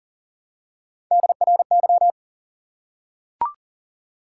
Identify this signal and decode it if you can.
Morse code, heard as a keyed tone, about 40 wpm, 700 Hz. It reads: DRY
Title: Three Letter Words - 40wpm